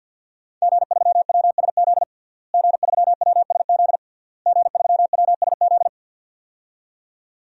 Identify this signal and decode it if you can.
Morse code, heard as a keyed tone, about 51 wpm, 700 Hz. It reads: G3WSZ G3WSZ G3WSZ
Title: DXCC-100 Call Signs - 5 Characters - Repeated 3x - 50wpm